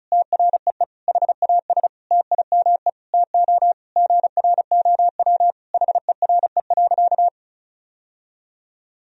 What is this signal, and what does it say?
Morse code, heard as a keyed tone, about 36 wpm, 700 Hz. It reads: TREE HAS TIME TO GROW HERE.
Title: Sentences from Top 1000 Words - Review Part 1 - 35wpm